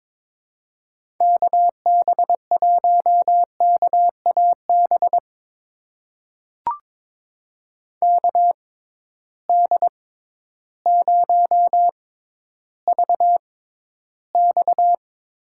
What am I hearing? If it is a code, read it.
Morse code, heard as a keyed tone, about 22 wpm, spaced as for 12 wpm, 700 Hz. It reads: KB1KAB KD0VX
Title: USA Call Signs - 6 Characters - 6x Character Spacing - 22wpm